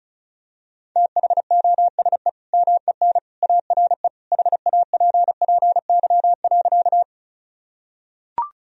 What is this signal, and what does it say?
Morse code, heard as a keyed tone, about 34 wpm, 700 Hz. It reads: THOSE MEN ARE HAPPY.
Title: Sentences from Top 1000 Words - Review Part 1 - 35wpm